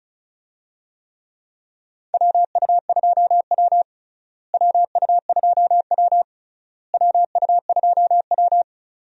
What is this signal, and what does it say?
Morse code, heard as a keyed tone, about 36 wpm, 700 Hz. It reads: WU2W WU2W WU2W
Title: USA Call Signs - 4 Characters - Repeated 3x - 35wpm